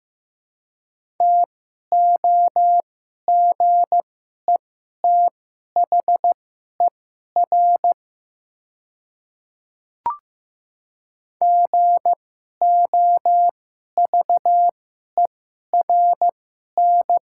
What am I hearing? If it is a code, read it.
Morse code, heard as a keyed tone, about 15 wpm, spaced as for 11 wpm, 700 Hz. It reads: TOGETHER GOVERN
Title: 2x Character Spacing - Top 500 Words - 15wpm